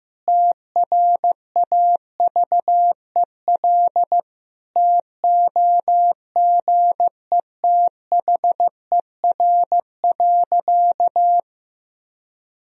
Morse code, heard as a keyed tone, about 16 wpm, 700 Hz. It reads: TRAVEL TOGETHER.
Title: Sentences from Top 1000 Words - Review Part 2 - 15wpm